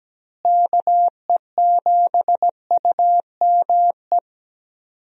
Morse code, heard as a keyed tone, about 17 wpm, 700 Hz. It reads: KE7UME
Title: USA Call Signs - 6 Characters - Repeated 2x - 17wpm